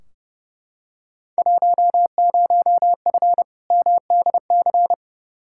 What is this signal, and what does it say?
Morse code, heard as a keyed tone, about 30 wpm, 700 Hz. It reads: 10F MDC
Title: Field Day Exchanges Encore - 30wpm